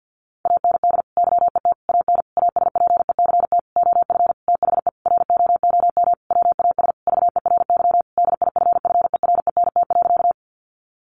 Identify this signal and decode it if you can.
Morse code, heard as a keyed tone, about 50 wpm, 700 Hz. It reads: AND YET AN ASPECT OF THE ROOM WAS VERY DIFFERENT.
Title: Sentences from Top 1300 Words 50wpm